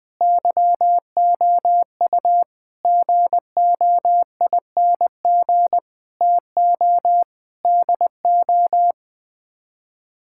Morse code, heard as a keyed tone, about 21 wpm, 700 Hz. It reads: YOU GOING TO DO
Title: Sets of 6 Words - 20wpm